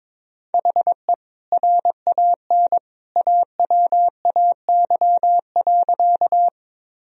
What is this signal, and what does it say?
Morse code, heard as a keyed tone, about 22 wpm, 700 Hz. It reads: HE RAN AWAY.
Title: Sentences from Top 1000 Words - Review Part 3 - 22wpm